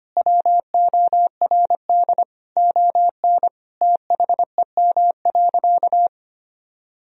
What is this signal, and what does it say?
Morse code, heard as a keyed tone, about 25 wpm, 690 Hz. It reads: WORD ON THEM.
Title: Sentences from Top 1000 Words - Review Part 4 - 25wpm